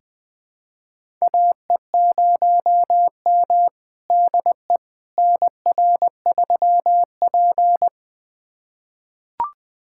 Morse code, heard as a keyed tone, about 20 wpm, 700 Hz. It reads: AE0M DE NR3P
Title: USA Callsign DE Callsign - 4 Characters - 20wpm